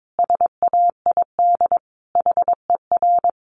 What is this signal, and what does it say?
Morse code, heard as a keyed tone, about 22 wpm, 700 Hz. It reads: SAID HER
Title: Sentences from Top 700 Words 22wpm